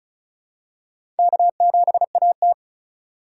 Morse code, heard as a keyed tone, about 36 wpm, 700 Hz. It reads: K7AT
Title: USA Call Signs - 4 Characters - 4x Character Spacing - 35wpm